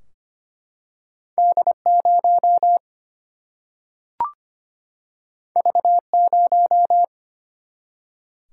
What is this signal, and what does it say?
Morse code, heard as a keyed tone, about 25 wpm, 700 Hz. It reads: D0 V0
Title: Single-Letter Single-Number 25wpm